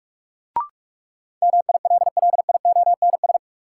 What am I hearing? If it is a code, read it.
Morse code, heard as a keyed tone, about 44 wpm, 700 Hz. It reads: MILLIONS
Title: Sets of 5 Words - New Word Drill - 45wpm